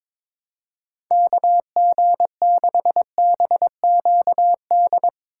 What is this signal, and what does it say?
Morse code, heard as a keyed tone, about 22 wpm, 700 Hz. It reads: KG6BQD
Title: USA Call Signs - 6 Characters - 6x Character Spacing - 22wpm